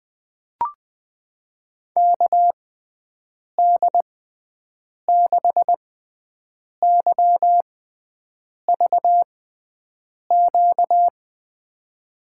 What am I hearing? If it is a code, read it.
Morse code, heard as a keyed tone, about 20 wpm, spaced as for 7 wpm, 700 Hz. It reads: KD6YVQ
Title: USA Call Signs - 6 Characters - 6x Character Spacing - 20wpm